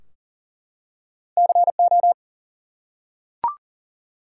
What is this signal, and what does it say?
Morse code, heard as a keyed tone, about 40 wpm, 700 Hz. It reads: CO
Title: US States and Canadian Provinces 40wpm